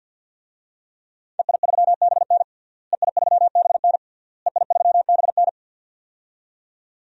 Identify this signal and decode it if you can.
Morse code, heard as a keyed tone, about 53 wpm, 700 Hz. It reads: EI3BN EI3BN EI3BN
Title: DXCC-100 Call Signs - 5 Characters - Repeated 3x - 50wpm